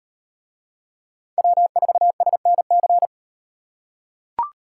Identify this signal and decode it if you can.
Morse code, heard as a keyed tone, about 38 wpm, 700 Hz. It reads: W4SNC